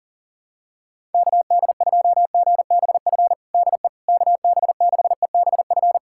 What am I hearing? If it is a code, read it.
Morse code, heard as a keyed tone, about 40 wpm, 700 Hz. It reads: KD2GBF DE KB6EBF
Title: USA Callsign DE Callsign 40wpm